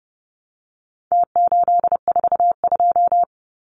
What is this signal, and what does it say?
Morse code, heard as a keyed tone, about 30 wpm, 700 Hz. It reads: T842
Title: Numbers - Contest Style - 4-Digit 30wpm